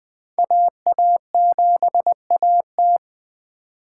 Morse code, heard as a keyed tone, about 20 wpm, 700 Hz. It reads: AA7AT
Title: Speed Racing - USA 2x2 Call Signs 17wpm